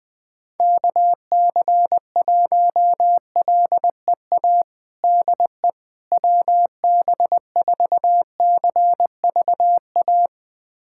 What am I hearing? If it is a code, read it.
Morse code, heard as a keyed tone, about 20 wpm, 700 Hz. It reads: KC1LEA DE WB4CVA
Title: USA Callsign DE Callsign 20wpm